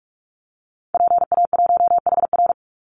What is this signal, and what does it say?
Morse code, heard as a keyed tone, about 44 wpm, 700 Hz. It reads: PA1HR
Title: Call Signs Repeated 2x - 45wpm